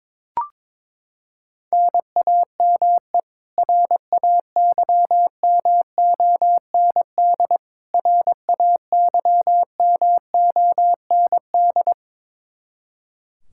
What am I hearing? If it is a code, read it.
Morse code, heard as a keyed tone, about 21 wpm, 700 Hz. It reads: NAME RAYMOND RAYMOND
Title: Practice QSO - Exchange 1 - Beginning - 22wpm